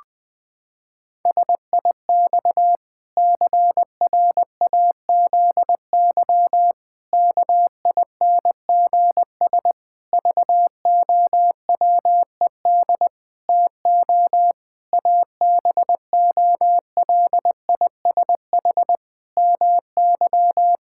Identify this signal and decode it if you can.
Morse code, heard as a keyed tone, about 20 wpm, 700 Hz. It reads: SIX CRAZY KINGS VOWED TO ABOLISH MY
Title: Pangram Challenge v2 - 20wpm